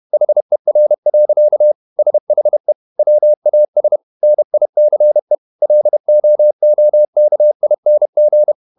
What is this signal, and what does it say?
Morse code, heard as a keyed tone, about 31 wpm, 600 Hz. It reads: HER. SHE WAS NICE LOOKING
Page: Never 31wpm | CW med Gnister